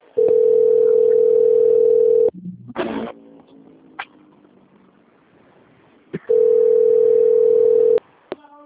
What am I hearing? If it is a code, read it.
Ringback tone (North America), 440 + 480 Hz.